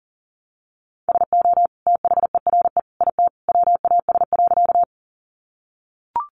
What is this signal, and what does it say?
Morse code, heard as a keyed tone, about 40 wpm, 700 Hz. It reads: SO THERE IT WAS.
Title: Sentences from Top 100 Words 40wpm